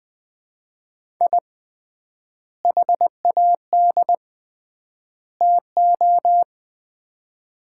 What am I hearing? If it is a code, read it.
Morse code, heard as a keyed tone, about 20 wpm, 700 Hz. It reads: I HAD TO
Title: 3x Word-Spacing - Sets of 3 Words - 20wpm